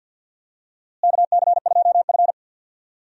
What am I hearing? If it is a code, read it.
Morse code, heard as a keyed tone, about 52 wpm, 700 Hz. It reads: KC2F
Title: USA Call Signs - 4 Characters - Repeated 2x - 50wpm